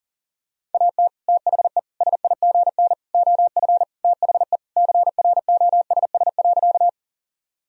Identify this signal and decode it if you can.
Morse code, heard as a keyed tone, about 40 wpm, 700 Hz. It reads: AT THE SIGN OF THE CROSS.
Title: Sentences from Top 1000 Words - Review Part 1 - 40wpm